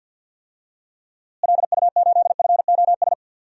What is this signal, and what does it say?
Morse code, heard as a keyed tone, about 51 wpm, 700 Hz. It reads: LU8FOS
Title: DXCC-100 Call Signs - 6 Characters - Repeated 3x - 50wpm